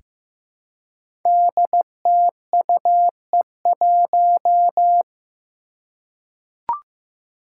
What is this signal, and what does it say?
Morse code, heard as a keyed tone, about 15 wpm, 700 Hz. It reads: DTUE1
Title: Five Challenging Letter-Number Combinations 15wpm